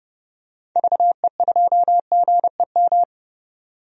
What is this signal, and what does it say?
Morse code, heard as a keyed tone, about 30 wpm, 700 Hz. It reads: VE2GEM
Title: DXCC-100 Call Signs - 6 Characters - 30wpm